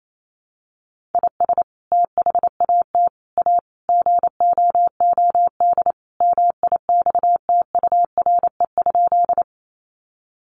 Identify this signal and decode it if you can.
Morse code, heard as a keyed tone, about 28 wpm, 700 Hz. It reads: IS THAT A GOOD MIXTURE?
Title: Sentences from Top 1500 Words 28wpm